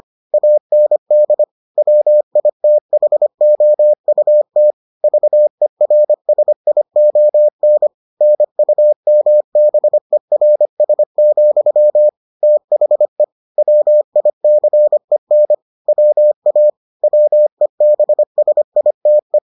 Morse code, heard as a keyed tone, about 25 wpm, 600 Hz. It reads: AND WITHOUT VERSION NUMBERS, THE WICEN WA WEBSITE